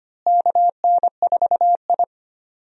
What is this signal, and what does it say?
Morse code, heard as a keyed tone, about 25 wpm, 700 Hz. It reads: KN4I
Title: USA Call Signs - 4 Characters - Repeated 3x - 25wpm